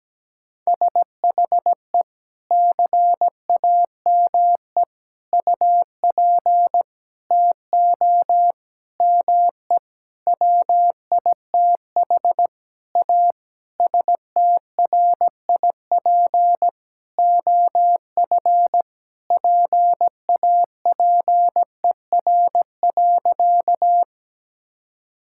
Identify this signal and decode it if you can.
Morse code, heard as a keyed tone, about 17 wpm, 700 Hz. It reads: SHE CAME UP TO ME WITH A STRIP OF PAPER.
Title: Sentences from Top 1600 Words 17wpm